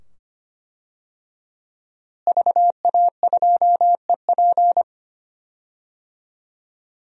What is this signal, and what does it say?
Morse code, heard as a keyed tone, about 25 wpm, 700 Hz. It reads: VA2EP
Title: DXCC-100 Call Signs - 5 Characters - Repeated 2x - 25wpm